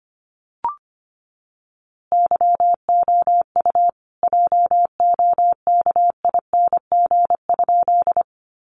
Morse code, heard as a keyed tone, about 25 wpm, 700 Hz. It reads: YOU JOKING?
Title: A Sound Of Thunder 25wpm